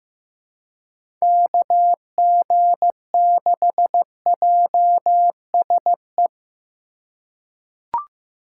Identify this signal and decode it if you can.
Morse code, heard as a keyed tone, about 15 wpm, 700 Hz. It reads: KG6JSE
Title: USA Call Signs - 4x Character Spacing - 15wpm